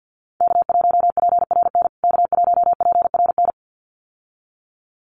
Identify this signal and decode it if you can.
Morse code, heard as a keyed tone, about 50 wpm, 700 Hz. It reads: K1PRN K1PRN
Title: USA Call Signs Repeated 2x 50wpm